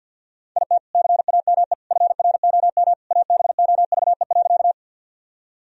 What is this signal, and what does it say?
Morse code, heard as a keyed tone, about 50 wpm, 700 Hz. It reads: IT CAME FROM ABOVE.
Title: Sentences from Top 1000 Words - Review Part 2 - 50wpm